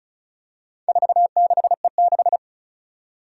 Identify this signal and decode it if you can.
Morse code, heard as a keyed tone, about 34 wpm, 700 Hz. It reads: 46E6
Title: Four Challenging Letter-Number Combinations 35wpm